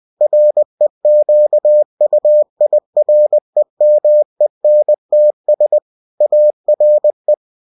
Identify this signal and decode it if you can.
Morse code, heard as a keyed tone, about 20 wpm, 600 Hz. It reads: REQUIREMENTS ARE